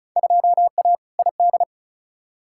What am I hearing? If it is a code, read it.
Morse code, heard as a keyed tone, about 34 wpm, 700 Hz. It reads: 2A ID
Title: Field Day Exchanges - 35wpm